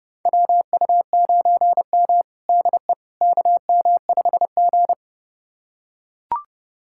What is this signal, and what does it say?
Morse code, heard as a keyed tone, about 30 wpm, 700 Hz. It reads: WU9M DE KM5G
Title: USA Callsign DE Callsign - 4 Characters - 30wpm